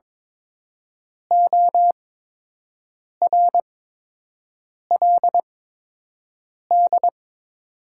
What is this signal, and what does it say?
Morse code, heard as a keyed tone, about 22 wpm, spaced as for 6 wpm, 700 Hz. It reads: ORLD
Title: 8x Character Spacing - Top 200 Words - 22wpm